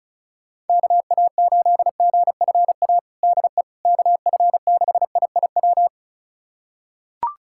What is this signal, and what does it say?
Morse code, heard as a keyed tone, about 36 wpm, 700 Hz. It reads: KA8GFA DE KF6IIW
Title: USA Callsign DE Callsign 35wpm